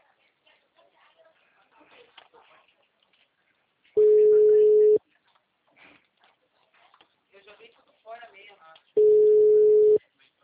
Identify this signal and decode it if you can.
Ringback tone (Europe, 425 Hz), with speech.
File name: out- -2002- - - .5723